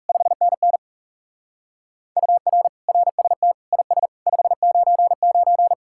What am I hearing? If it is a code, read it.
Morse code, heard as a keyed tone, about 34 wpm, 700 Hz. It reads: 5NN UR RST IS 599
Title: Speed-Racing Common QSO Phrases Encore - 30wpm